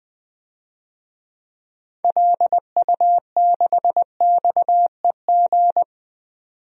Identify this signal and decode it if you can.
Morse code, heard as a keyed tone, about 20 wpm, 700 Hz. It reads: LU6XEG
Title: DXCC-100 Call Signs - 6 Characters - 20wpm